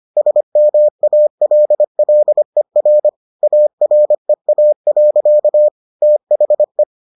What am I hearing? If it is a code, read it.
Morse code, heard as a keyed tone, about 25 wpm, 600 Hz. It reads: SMALLER AREA. THE